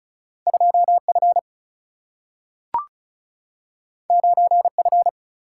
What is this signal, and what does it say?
Morse code, heard as a keyed tone, about 36 wpm, 700 Hz. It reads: 2F 9F
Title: Single-Letter Single-Number 35wpm